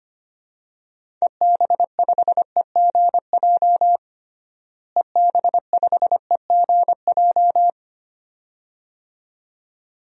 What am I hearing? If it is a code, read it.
Morse code, heard as a keyed tone, about 25 wpm, 700 Hz. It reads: EB5EGJ EB5EGJ
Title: DXCC-100 Call Signs - 6 Characters - Repeated 2x - 25wpm